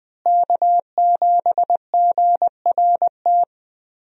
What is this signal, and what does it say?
Morse code, heard as a keyed tone, about 20 wpm, 700 Hz. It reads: K7GRT
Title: USA Call Signs - 5 Characters - Repeated 3x - 20wpm